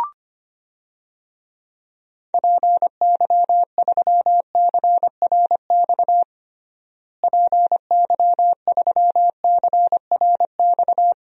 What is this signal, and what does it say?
Morse code, heard as a keyed tone, about 25 wpm, 700 Hz. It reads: PY3CRX PY3CRX
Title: DXCC-100 Call Signs - 6 Characters - Repeated 3x - 25wpm